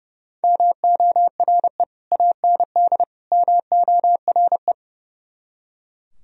Morse code, heard as a keyed tone, about 37 wpm, 700 Hz. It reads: MORE AND MORE
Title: Speed-Racing - Sets of 3 Words 30wpm